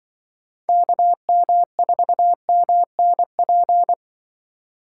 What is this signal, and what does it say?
Morse code, heard as a keyed tone, about 24 wpm, 700 Hz. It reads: KM4MNP
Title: Speed Racing - USA Call Signs 20wpm